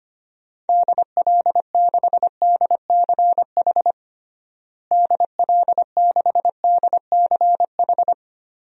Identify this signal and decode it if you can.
Morse code, heard as a keyed tone, about 25 wpm, 700 Hz. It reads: DL6DCH DL6DCH
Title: DXCC-100 Call Signs - 6 Characters - Repeated 3x - 25wpm